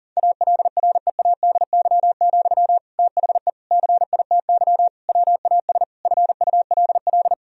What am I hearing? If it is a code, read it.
Morse code, heard as a keyed tone, about 40 wpm, 700 Hz. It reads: ALREADY, THE CITY WAS FULL
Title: Sentences from Top 1000 Words - Review Part 4 - 40wpm